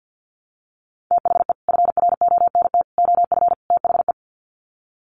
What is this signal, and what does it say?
Morse code, heard as a keyed tone, about 50 wpm, 700 Hz. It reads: THE FRONT OF THE
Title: Sets of 4 Words 50wpm